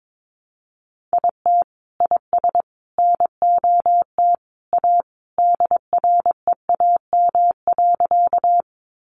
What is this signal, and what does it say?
Morse code, heard as a keyed tone, about 22 wpm, 700 Hz. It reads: IT IS NOT A DREAM.
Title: Sentences from Top 1500 Words 22wpm